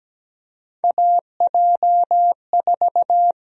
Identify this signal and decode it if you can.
Morse code, heard as a keyed tone, about 17 wpm, 700 Hz. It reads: AJ4
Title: USA Call Sign Prefixes - 17wpm